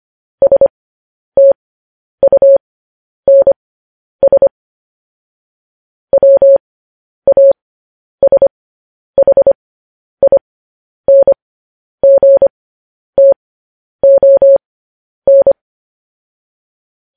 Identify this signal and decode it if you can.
Morse code, heard as a keyed tone, about 25 wpm, spaced as for 10 wpm, 550 Hz. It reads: STUNS WASHINGTON
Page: news-headlines-in-morse-code-at-10-wpm